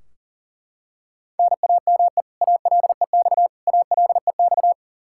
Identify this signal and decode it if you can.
Morse code, heard as a keyed tone, about 40 wpm, 700 Hz. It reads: NAME ALEX ALEX
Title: Practice QSO - Exchange 1 - Beginning - 40wpm